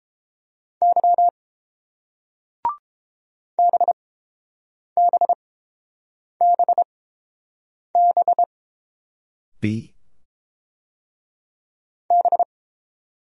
Morse code, heard as a keyed tone, about 31 wpm, spaced as for 7 wpm, 690 Hz. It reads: Y BBBB B